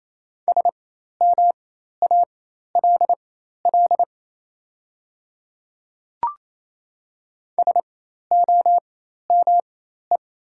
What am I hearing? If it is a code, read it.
Morse code, heard as a keyed tone, about 27 wpm, spaced as for 13 wpm, 690 Hz. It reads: SMALL SOME
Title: 4x Character Spacing - Top 200 Words - 28wpm